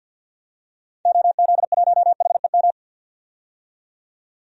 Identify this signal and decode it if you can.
Morse code, heard as a keyed tone, about 48 wpm, 700 Hz. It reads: OZ1HEM
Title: DXCC-100 Call Signs - 6 Characters - 50wpm